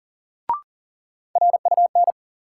Morse code, heard as a keyed tone, about 39 wpm, 700 Hz. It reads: RUN
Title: Sentences from Top 300 Words - New Word Drill - 40wpm